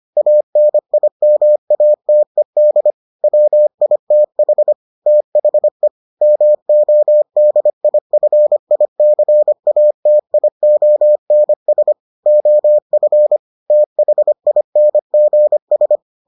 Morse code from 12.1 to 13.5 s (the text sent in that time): OF